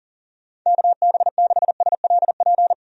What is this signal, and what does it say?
Morse code, heard as a keyed tone, about 37 wpm, 700 Hz. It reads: KB6SLP
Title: USA Call Signs - 6 Characters - 40wpm